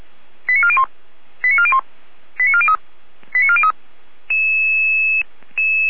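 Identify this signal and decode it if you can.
Five-tone selective call (ZVEI), heard as quick runs of five tones, tones of about 70 ms. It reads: 89401 89401 89403 89403
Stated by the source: ZVEI five-tone call received by SDR, recorded by feuerwehr-orlamuende.org fire brigade.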